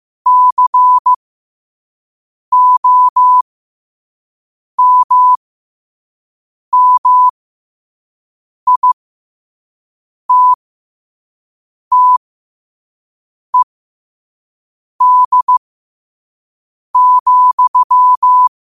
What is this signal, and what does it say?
Morse code, heard as a keyed tone, about 15 wpm, spaced as for 5 wpm, 1000 Hz. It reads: COMMITTED,
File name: qotdfiles / 5wpm